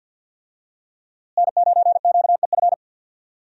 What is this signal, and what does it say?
Morse code, heard as a keyed tone, about 53 wpm, 700 Hz. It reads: N9QEF
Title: USA Call Signs - 5 Characters - Repeated 2x - 50wpm